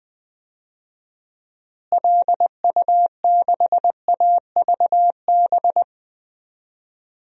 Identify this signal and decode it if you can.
Morse code, heard as a keyed tone, about 20 wpm, 700 Hz. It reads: LU6AVB
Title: DXCC-100 Call Signs - 6 Characters - Repeated 3x - 20wpm